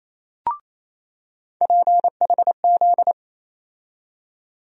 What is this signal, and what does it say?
Morse code, heard as a keyed tone, about 28 wpm, 700 Hz. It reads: PHZ
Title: USA Call Sign Postfixes - 3 Letters 28wpm